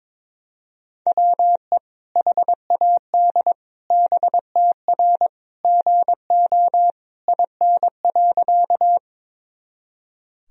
Morse code, heard as a keyed tone, about 22 wpm, 700 Hz. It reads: WE HAD BTR GO IN.
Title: Sentences from Top 400 Words and Abbreviations 22wpm